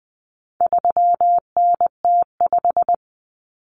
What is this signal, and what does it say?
Morse code, heard as a keyed tone, about 20 wpm, 700 Hz. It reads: 3NT5
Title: Common Cut Numbers - 4-Digit 20wpm